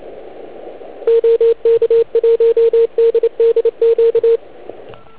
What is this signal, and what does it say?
Morse code, heard as a keyed tone, about 25 wpm, 460 Hz, with radio noise. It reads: OK1DDQ